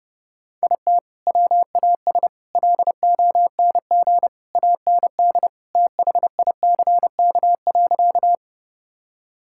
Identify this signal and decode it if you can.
Morse code, heard as a keyed tone, about 32 wpm, 700 Hz. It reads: IT WAS LONG AND THICK.
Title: Sentences from Top 1600 Words 30wpm